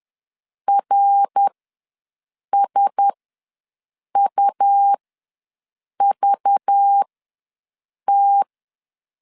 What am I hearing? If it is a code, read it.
Morse code, heard as a keyed tone, about 10 wpm, spaced as for 6 wpm, 800 Hz. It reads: RSUVT
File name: LEZIONI / 6 PKX0 / 6 P ws 10wpm 3751